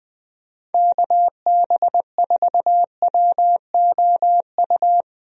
Morse code, heard as a keyed tone, about 20 wpm, 700 Hz. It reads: KB4WOU
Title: USA Call Signs - 6 Characters - 20wpm